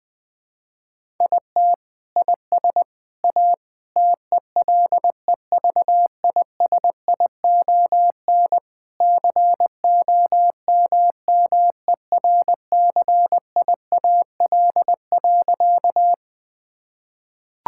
Morse code, heard as a keyed tone, about 20 wpm, 700 Hz. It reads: IT IS A TELEVISION COMMERCIAL.
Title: Sentences from Top 1000 Words - Review Part 5 - 20wpm